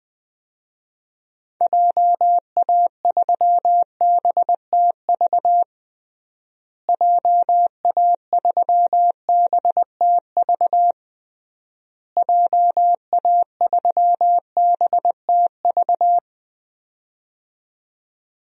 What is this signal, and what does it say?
Morse code, heard as a keyed tone, about 21 wpm, 700 Hz. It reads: JA3BTV JA3BTV JA3BTV
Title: DXCC-100 Call Signs - 6 Characters - Repeated 3x - 20wpm